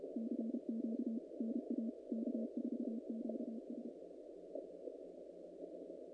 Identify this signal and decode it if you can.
Morse code, heard as a keyed tone, about 33 wpm, 220 Hz. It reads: CQ NA K4QS
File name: contests / naqp-cw-k5tr-op-k5pi- -2100